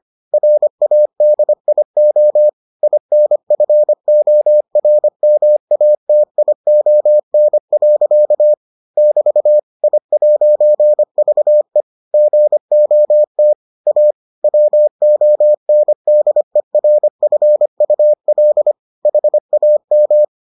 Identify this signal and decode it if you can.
Morse code, heard as a keyed tone, about 25 wpm, 600 Hz. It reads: RADIO INFORMATION. <BT> I'VE GOT A WONDERFUL HAM